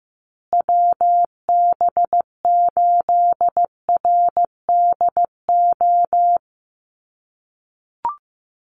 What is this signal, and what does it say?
Morse code, heard as a keyed tone, about 15 wpm, 700 Hz. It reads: WB8RDO
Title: USA Call Signs Repeated 2x 15wpm